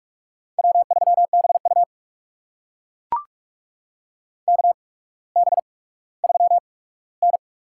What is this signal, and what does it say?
Morse code, heard as a keyed tone, about 44 wpm, spaced as for 19 wpm, 700 Hz. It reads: W3BU KB3N
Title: USA Call Signs - 4 Characters - 8x Character Spacing - 45wpm